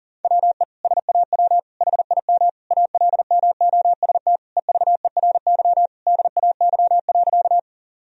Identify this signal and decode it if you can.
Morse code, heard as a keyed tone, about 37 wpm, 700 Hz. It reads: WE SAW HIM ALMOST EVERY DAY.
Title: Sentences from Top 1000 Words - Review Part 4 - 40wpm